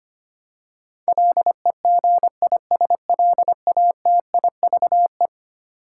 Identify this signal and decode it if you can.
Morse code, heard as a keyed tone, about 25 wpm, 700 Hz. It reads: LEGISLATIVE
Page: morse-code-ninja-practice-25wpm